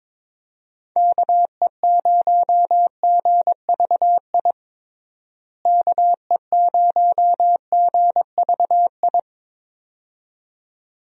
Morse code, heard as a keyed tone, about 22 wpm, 700 Hz. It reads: KE0GVI KE0GVI
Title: USA Call Signs - 6 Characters - Repeated 2x - 22wpm